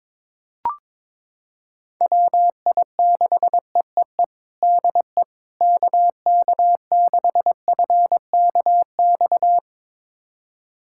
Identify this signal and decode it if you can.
Morse code, heard as a keyed tone, about 22 wpm, 700 Hz. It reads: WI6EEE DE KK6FKX
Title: USA Callsign DE Callsign - 6 Characters - 22wpm